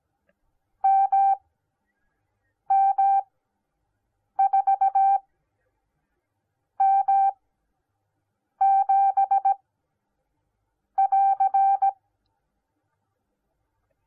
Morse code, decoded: MM4M7<AR>

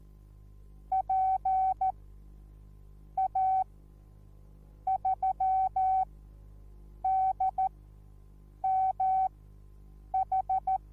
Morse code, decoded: PA3DMH

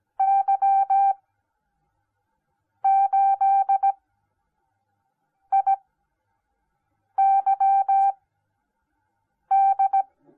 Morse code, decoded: Y8IYD

